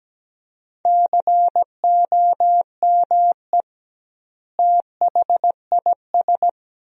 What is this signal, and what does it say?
Morse code, heard as a keyed tone, about 17 wpm, 700 Hz. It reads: COME THIS
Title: 2x Word-Spacing - Sentences from Top 100 Words - 17wpm